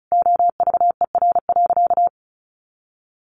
Morse code, heard as a keyed tone, about 34 wpm, 700 Hz. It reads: OVER.
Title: Sentences from Top 200 Words 35wpm